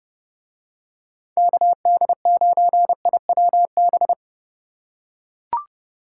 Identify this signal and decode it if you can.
Morse code, heard as a keyed tone, about 30 wpm, 700 Hz. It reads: KD9IWB